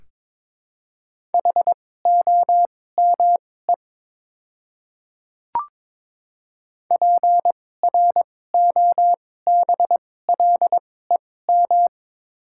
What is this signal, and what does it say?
Morse code, heard as a keyed tone, about 22 wpm, spaced as for 16 wpm, 700 Hz. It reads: HOME PROBLEM
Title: 2x Character Spacing - Top 500 Words - 22wpm